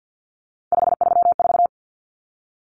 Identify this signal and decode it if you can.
Morse code, heard as a keyed tone, about 50 wpm, 700 Hz. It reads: 534
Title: Common Cut Numbers - 3-Digit 50wpm